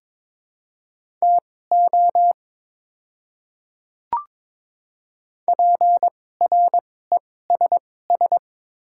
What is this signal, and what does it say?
Morse code, heard as a keyed tone, about 22 wpm, spaced as for 16 wpm, 690 Hz. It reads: TO PRESS
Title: 2x Character Spacing - Top 300 Words - 22wpm